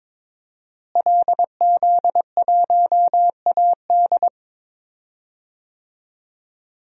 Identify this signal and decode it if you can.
Morse code, heard as a keyed tone, about 22 wpm, 700 Hz. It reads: LZ1AD
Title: DXCC-100 Call Signs - 5 Characters - Repeated 2x - 22wpm